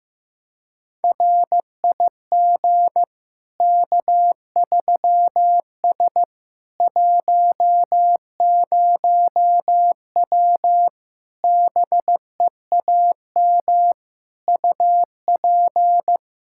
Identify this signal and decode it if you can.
Morse code, heard as a keyed tone, about 15 wpm, 700 Hz. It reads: RIG K3S 10W BEAM UP
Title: Practice QSO - Exchange 2 - Info - 15wpm